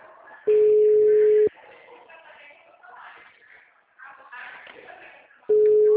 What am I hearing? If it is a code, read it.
Ringback tone (Europe, 425 Hz), with speech.